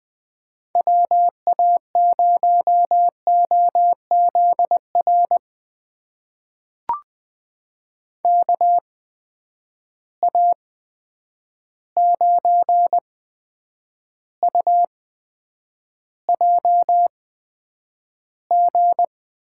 Morse code, heard as a keyed tone, about 20 wpm, spaced as for 9 wpm, 700 Hz. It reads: WA0OZR KA9UJG
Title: USA Call Signs - 6 Characters - 8x Character Spacing - 20wpm